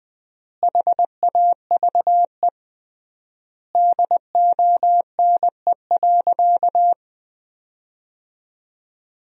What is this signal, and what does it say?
Morse code, heard as a keyed tone, about 20 wpm, 700 Hz. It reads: HAVE DONE.
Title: 3x Word-Spacing - Sentences from Top 500 Words - 20wpm